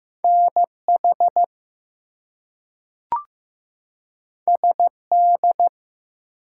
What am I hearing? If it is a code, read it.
Morse code, heard as a keyed tone, about 15 wpm, 700 Hz. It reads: NH SD
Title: US States and Canadian Provinces 15wpm